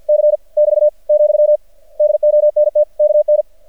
Morse code, heard as a keyed tone, about 53 wpm, 600 Hz. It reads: C <BT> , DONT KN
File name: archived / CRSnet